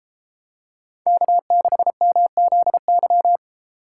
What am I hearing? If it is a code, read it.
Morse code, heard as a keyed tone, about 33 wpm, 700 Hz. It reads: K6MZY